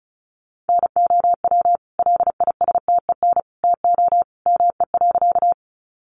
Morse code, heard as a keyed tone, about 34 wpm, 700 Hz. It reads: NOW LISTEN TO ME.
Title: Sentences from Top 400 Words 35wpm